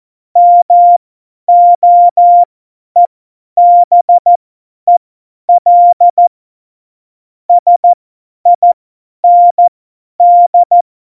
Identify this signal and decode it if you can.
Morse code, heard as a keyed tone, about 14 wpm, spaced as for 10 wpm, 700 Hz. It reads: MOEBEL SIND
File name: Morsekurs / Witze / Witz 03. September 2019, 50 F70